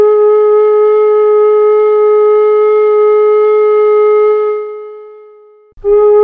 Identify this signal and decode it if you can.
Dial tone (Europe, 425 Hz).